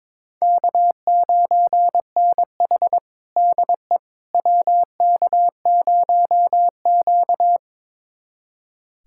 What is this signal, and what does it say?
Morse code, heard as a keyed tone, about 22 wpm, 700 Hz. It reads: K9NH DE WK0Q
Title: USA Callsign DE Callsign - 4 Characters - 22wpm